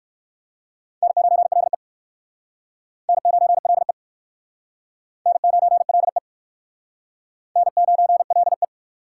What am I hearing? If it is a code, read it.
Morse code, heard as a keyed tone, about 56 wpm, 700 Hz. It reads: N9LE N9LE N9LE N9LE
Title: Speed Racing - USA 1x2 Call Signs 45wpm